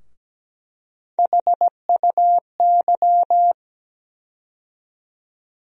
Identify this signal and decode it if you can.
Morse code, heard as a keyed tone, about 17 wpm, 700 Hz. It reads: HUY
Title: USA Call Sign Postfixes - 3 Letters 17wpm